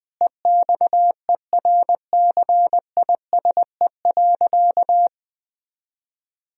Morse code, heard as a keyed tone, about 20 wpm, 700 Hz. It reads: EXERCISE.
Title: Sentences from Top 1000 Words - Review Part 4 - 20wpm